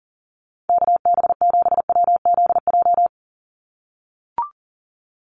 Morse code, heard as a keyed tone, about 40 wpm, 690 Hz. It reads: KB7WZJ
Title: USA Call Signs 40wpm